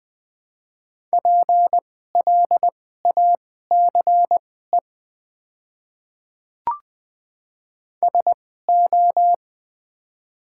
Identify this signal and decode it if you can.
Morse code, heard as a keyed tone, about 20 wpm, spaced as for 14 wpm, 700 Hz. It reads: PLACE SO
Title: 2x Character Spacing - Top 300 Words - 20wpm